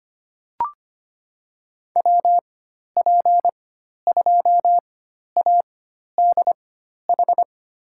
Morse code, heard as a keyed tone, about 25 wpm, spaced as for 12 wpm, 700 Hz. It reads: WP2ADH
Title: USA Call Signs - 6 Characters - 4x Character Spacing - 25wpm